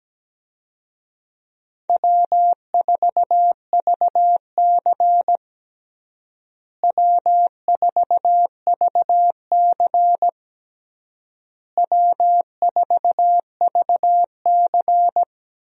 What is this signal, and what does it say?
Morse code, heard as a keyed tone, about 17 wpm, 700 Hz. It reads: W4VC W4VC W4VC
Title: USA Call Signs - 4 Characters - Repeated 3x - 17wpm